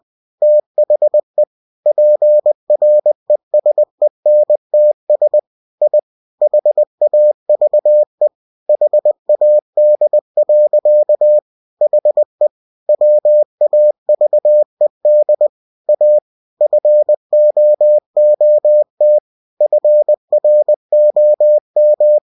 Morse code, heard as a keyed tone, about 20 wpm, 600 Hz. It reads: THE PRESENTS I HAVE HAD. HE WAVED A FOOT FROM
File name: cw-books / winnie-the-pooh / winnie-20wpm-page-0047